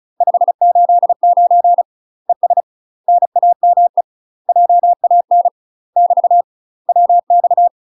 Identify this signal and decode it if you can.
Morse code, heard as a keyed tone, about 36 wpm, 700 Hz. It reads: 589 ES NAME JAN <BT> WX
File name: UK MASTER QSOs TRAINING / 35WPM / UK MASTER QSOs 0022